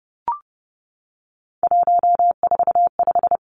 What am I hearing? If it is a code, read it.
Morse code, heard as a keyed tone, about 30 wpm, 700 Hz. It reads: 145
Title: Numbers - 3-Digit 30wpm